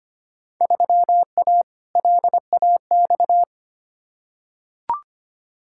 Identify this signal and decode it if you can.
Morse code, heard as a keyed tone, about 25 wpm, 700 Hz. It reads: 3A LAX
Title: Field Day Exchanges - 25wpm